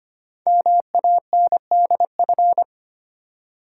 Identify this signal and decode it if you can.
Morse code, heard as a keyed tone, about 25 wpm, 690 Hz. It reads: MANDF
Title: Code Groups - 5 Letters - 25wpm